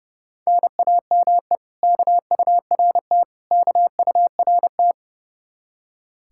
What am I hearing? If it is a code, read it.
Morse code, heard as a keyed tone, about 37 wpm, 700 Hz. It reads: NAME KURT KURT
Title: Speed-Racing Common QSO Phrases Encore - 30wpm